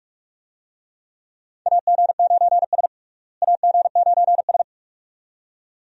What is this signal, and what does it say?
Morse code, heard as a keyed tone, about 42 wpm, 700 Hz. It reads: AG9S AG9S
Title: USA Call Signs - 4 Characters - Repeated 2x - 45wpm